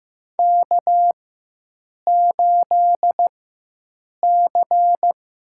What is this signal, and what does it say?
Morse code, heard as a keyed tone, about 15 wpm, spaced as for 7 wpm, 700 Hz. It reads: K8C
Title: USA Call Signs - 4x Character Spacing - 15wpm